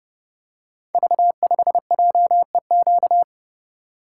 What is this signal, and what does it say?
Morse code, heard as a keyed tone, about 30 wpm, 700 Hz. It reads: V5JEQ
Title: Code Groups - 5 Letter:Numbers - 30wpm